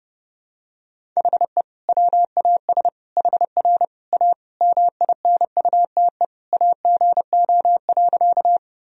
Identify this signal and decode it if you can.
Morse code, heard as a keyed tone, about 30 wpm, 700 Hz. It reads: HE WAS HR A MINUTE AGO.
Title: Sentences from Top 500 Words and Abbreviations 30wpm